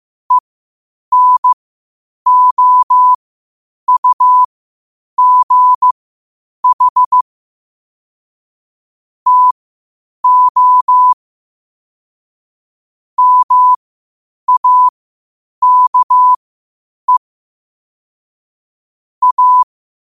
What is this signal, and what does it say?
Morse code, heard as a keyed tone, about 15 wpm, spaced as for 8 wpm, 1000 Hz. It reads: ENOUGH TO MAKE A
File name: qotdfiles / 8wpm